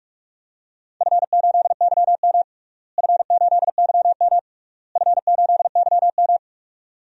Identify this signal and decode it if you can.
Morse code, heard as a keyed tone, about 46 wpm, 690 Hz. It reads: F8YM F8YM F8YM
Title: DXCC-100 Call Signs - 4 Characters - Repeated 3x - 45wpm